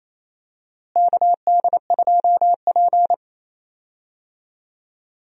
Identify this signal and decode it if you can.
Morse code, heard as a keyed tone, about 28 wpm, 700 Hz. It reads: KD2P
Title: USA Call Signs - 4 Characters - Repeated 3x - 28wpm